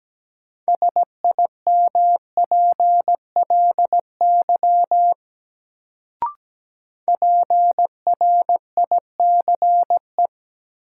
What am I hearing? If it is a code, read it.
Morse code, heard as a keyed tone, about 17 wpm, 700 Hz. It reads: SIMPLY PRICE
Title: Sentences from Top 600 Words - New Word Drill - 17wpm